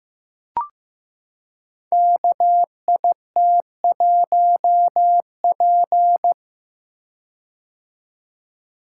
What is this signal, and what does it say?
Morse code, heard as a keyed tone, about 15 wpm, 700 Hz. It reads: KIT1P
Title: Code Groups - 5 Letter:Numbers - 15wpm